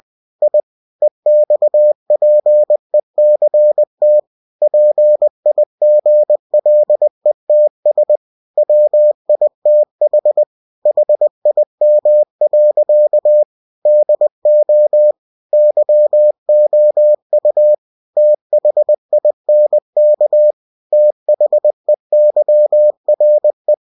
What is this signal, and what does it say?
Morse code, heard as a keyed tone, about 20 wpm, 600 Hz. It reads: I EXPECT PIGLETS WITH HIM. DO YOU THINK THEYRE
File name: cw-books / winnie-the-pooh / winnie-20wpm-page-0097